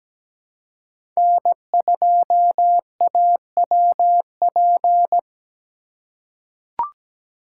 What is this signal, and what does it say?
Morse code, heard as a keyed tone, about 17 wpm, 700 Hz. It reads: N2AWP
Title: USA Call Signs - 5 Characters - 2x Character Spacing - 17wpm